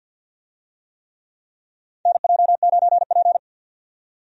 Morse code, heard as a keyed tone, about 52 wpm, 700 Hz. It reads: NJ9P
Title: USA Call Signs - 4 Characters - 50wpm